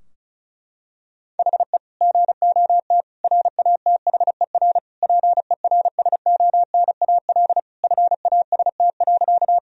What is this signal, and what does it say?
Morse code, heard as a keyed tone, about 34 wpm, 700 Hz. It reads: HE GOT RATHER PERSONAL FAST.
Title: Sentences from Top 1000 Words - Review Part 4 - 35wpm